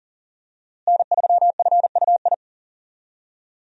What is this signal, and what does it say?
Morse code, heard as a keyed tone, about 40 wpm, 700 Hz. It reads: N3FUI
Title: USA Call Signs - 5 Characters - Repeated 2x - 40wpm